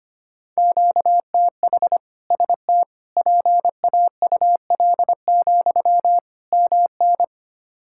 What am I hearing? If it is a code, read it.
Morse code, heard as a keyed tone, about 25 wpm, 700 Hz. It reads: QTH ST PAUL, MN
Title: K1USN SST QSO Practice 25wpm